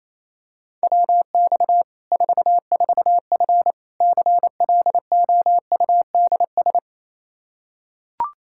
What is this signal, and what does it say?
Morse code, heard as a keyed tone, about 28 wpm, 700 Hz. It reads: WX 44F CLOUDS
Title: K1USN SST QSO Practice 28wpm